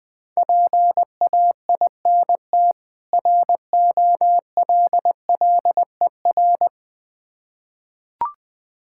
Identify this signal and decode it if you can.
Morse code, heard as a keyed tone, about 20 wpm, 690 Hz. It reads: PAINT ROLLER
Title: Difficult Words with Hints - 20wpm